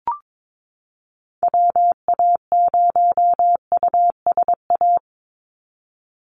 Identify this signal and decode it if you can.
Morse code, heard as a keyed tone, about 22 wpm, 700 Hz. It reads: WA0USA
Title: Call Signs Repeated 3x - 22wpm